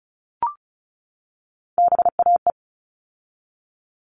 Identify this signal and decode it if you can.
Morse code, heard as a keyed tone, about 34 wpm, 700 Hz. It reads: BAE
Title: All Cut Numbers - 3-Digit 35wpm